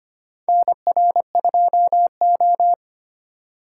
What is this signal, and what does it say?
Morse code, heard as a keyed tone, about 25 wpm, 700 Hz. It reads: NR2O
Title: USA Call Signs - 4 Characters - 2x Character Spacing - 25wpm